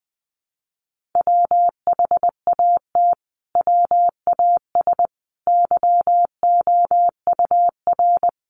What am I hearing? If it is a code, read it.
Morse code, heard as a keyed tone, about 20 wpm, 700 Hz. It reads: WHAT WAS YOUR
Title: Sentences from Top 1500 Words 20wpm